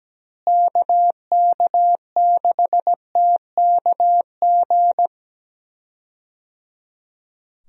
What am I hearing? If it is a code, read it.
Morse code, heard as a keyed tone, about 17 wpm, 700 Hz. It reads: KK6TKG
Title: USA Call Signs - 6 Characters - Repeated 3x - 17wpm